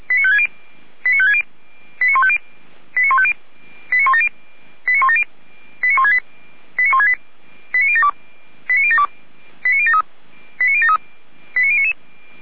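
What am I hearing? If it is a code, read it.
Five-tone selective call (ZVEI), heard as quick runs of five tones, tones of about 70 ms. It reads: 89560 89150 89170 89170 89168 89168 89072 89072 89073 89073 89099
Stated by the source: Five-tone ZVEI selective call received by SDR, from feuerwehr-orlamuende.org fire brigade archive.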